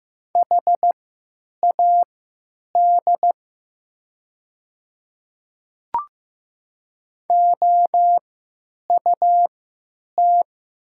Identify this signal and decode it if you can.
Morse code, heard as a keyed tone, about 15 wpm, spaced as for 8 wpm, 700 Hz. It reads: HAD OUT
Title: 3x Character Spacing - Top 100 Words - 15wpm